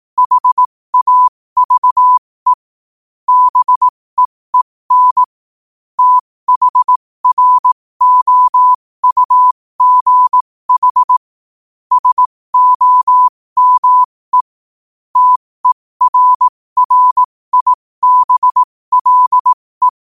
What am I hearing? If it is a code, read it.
Morse code, heard as a keyed tone, about 18 wpm, spaced as for 15 wpm, 1000 Hz. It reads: HAVE BEEN THROUGH SOME TERRIBLE
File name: qotdfiles / 15wpm